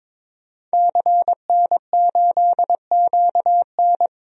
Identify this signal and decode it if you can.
Morse code, heard as a keyed tone, about 22 wpm, 700 Hz. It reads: CN8QN
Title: DXCC-100 Call Signs - 5 Characters - 22wpm